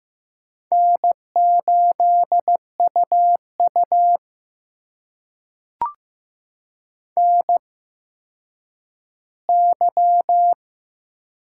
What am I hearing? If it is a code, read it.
Morse code, heard as a keyed tone, about 15 wpm, spaced as for 9 wpm, 700 Hz. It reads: N8UU NY
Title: USA Call Signs - 4 Characters - 8x Character Spacing - 15wpm